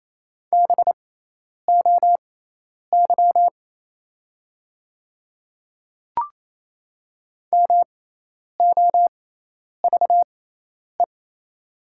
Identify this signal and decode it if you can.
Morse code, heard as a keyed tone, about 28 wpm, spaced as for 10 wpm, 700 Hz. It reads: BOY MOVE
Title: 6x Character Spacing - Top 200 Words - 28wpm